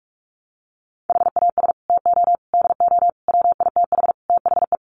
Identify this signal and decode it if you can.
Morse code, heard as a keyed tone, about 44 wpm, 700 Hz. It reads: HAS TO DO WITH THE
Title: Sets of 5 Words 45wpm